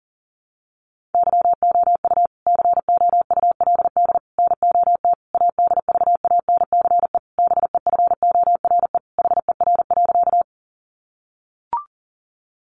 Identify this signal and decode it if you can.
Morse code, heard as a keyed tone, about 40 wpm, 700 Hz. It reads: YOU COULD NOT ADVANCE BEFORE HER.
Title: Sentences from Top 1400 Words 40wpm